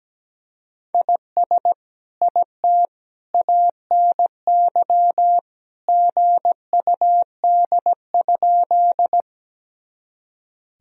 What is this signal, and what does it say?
Morse code, heard as a keyed tone, about 17 wpm, 700 Hz. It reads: IS IT ANY GUD?
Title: Sentences from Top 200 Words and Abbreviations 17wpm